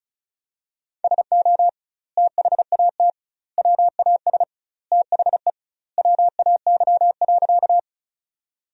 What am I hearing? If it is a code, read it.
Morse code, heard as a keyed tone, about 36 wpm, 700 Hz. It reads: SO THAT WAS THE WAY.
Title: 2x Word-Spacing - Sentences from Top 100 Words - 35wpm